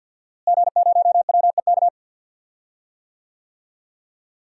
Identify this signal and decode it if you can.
Morse code, heard as a keyed tone, about 51 wpm, 700 Hz. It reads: G0WEK
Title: DXCC-100 Call Signs - 5 Characters - Repeated 2x - 50wpm